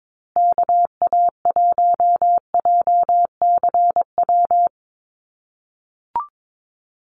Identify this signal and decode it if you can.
Morse code, heard as a keyed tone, about 22 wpm, 700 Hz. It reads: KA1JCW
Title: USA Call Signs Repeated 2x 22wpm